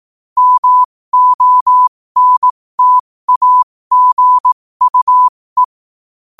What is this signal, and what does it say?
Morse code, heard as a keyed tone, about 18 wpm, spaced as for 15 wpm, 1000 Hz. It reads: MONTAGUE
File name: 15wpm